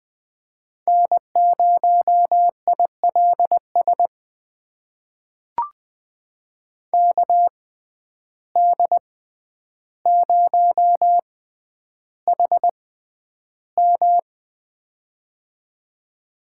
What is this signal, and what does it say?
Morse code, heard as a keyed tone, about 20 wpm, spaced as for 10 wpm, 700 Hz. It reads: N0ILS KD0HM
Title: USA Call Signs - 5 Characters - 6x Character Spacing - 20wpm